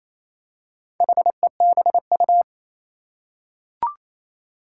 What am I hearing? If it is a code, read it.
Morse code, heard as a keyed tone, about 28 wpm, 700 Hz. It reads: HEBU
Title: Four Challenging Letter-Number Combinations 28wpm